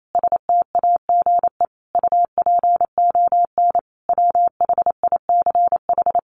Morse code, heard as a keyed tone, about 28 wpm, 700 Hz. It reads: STAGE UPON WHICH
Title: Sentences from Top 1200 Words 28wpm